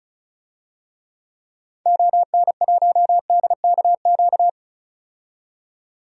Morse code, heard as a keyed tone, about 34 wpm, 700 Hz. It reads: ON1DKQ
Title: DXCC-100 Call Signs - 6 Characters - 35wpm